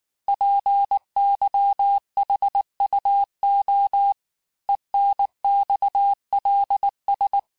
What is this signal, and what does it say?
Morse code, heard as a keyed tone, about 19 wpm, 800 Hz. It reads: PYHUO ENXLS
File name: cours-du-lundi files / cours-1583--texte-chiffre-18-mpm